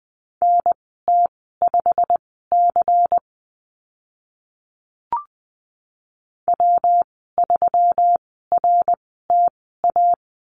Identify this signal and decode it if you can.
Morse code, heard as a keyed tone, about 20 wpm, spaced as for 14 wpm, 700 Hz. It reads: NT5C W3RTA
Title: 122 Lesson 20g - Number 3 - Call Signs 20wpm